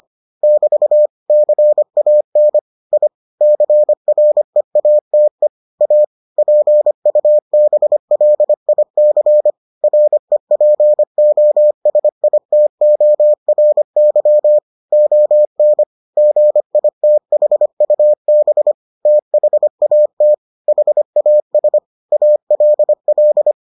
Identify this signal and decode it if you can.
Morse code, heard as a keyed tone, about 25 wpm, 600 Hz. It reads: <BT> CAN I CREATE A PUBLIC REPOSITORY ON GITHUB THAT HAS ALL